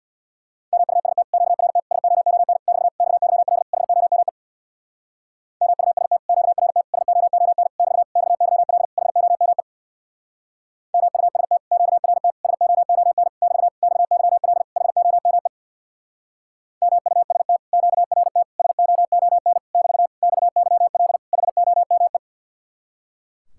Morse code, heard as a keyed tone, about 66 wpm, 700 Hz. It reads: MUST QRT SOON <BT> XYL HOME MUST QRT SOON <BT> XYL HOME MUST QRT SOON <BT> XYL HOME MUST QRT SOON <BT> XYL HOME